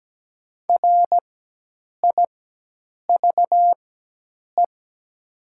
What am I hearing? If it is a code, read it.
Morse code, heard as a keyed tone, about 17 wpm, spaced as for 8 wpm, 700 Hz. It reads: RIVE
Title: 4x Character Spacing - Top 300 Words - 17wpm